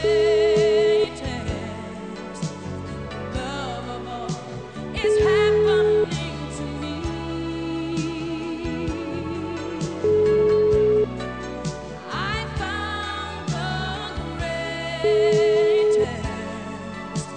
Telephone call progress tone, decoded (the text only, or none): ringback tone